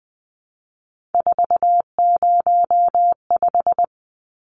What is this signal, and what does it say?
Morse code, heard as a keyed tone, about 20 wpm, 700 Hz. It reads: 405
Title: Numbers - 3-Digit 20wpm